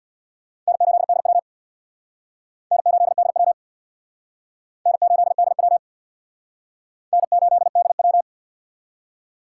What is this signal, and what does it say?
Morse code, heard as a keyed tone, about 62 wpm, 700 Hz. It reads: N8DW N8DW N8DW N8DW
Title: Speed Racing - USA Call Signs 50wpm